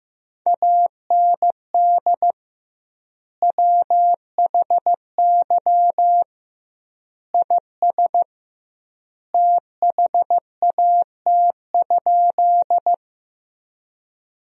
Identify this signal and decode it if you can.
Morse code, heard as a keyed tone, about 15 wpm, 700 Hz. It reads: AND WHY IS THAT?
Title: 2x Word-Spacing - Sentences from Top 200 Words - 15wpm